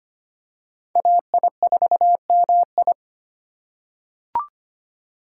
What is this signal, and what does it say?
Morse code, heard as a keyed tone, about 25 wpm, 700 Hz. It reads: AI4MI
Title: K1USN SST QSO Practice - 25wpm